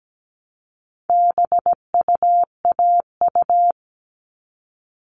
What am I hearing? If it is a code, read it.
Morse code, heard as a keyed tone, about 17 wpm, 700 Hz. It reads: BUAU
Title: All Cut Numbers - 4-Digit 17wpm